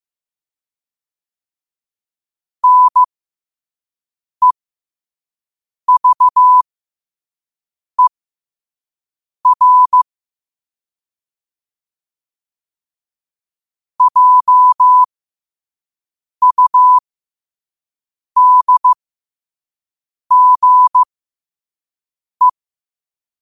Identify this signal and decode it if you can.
Morse code, heard as a keyed tone, about 15 wpm, spaced as for 5 wpm, 1000 Hz. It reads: NEVER JUDGE
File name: qotdfiles / 5wpm